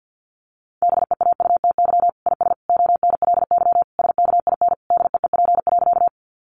Morse code, heard as a keyed tone, about 50 wpm, 700 Hz. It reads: BEAUTY IS ONLY SKIN DEEP.
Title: Sentences from Top 1300 Words 50wpm